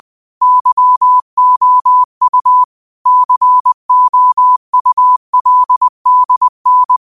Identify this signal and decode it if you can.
Morse code, heard as a keyed tone, about 20 wpm, 1000 Hz. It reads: YOU COULDN
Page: New quotes every day in morse code at 20 Words per minute.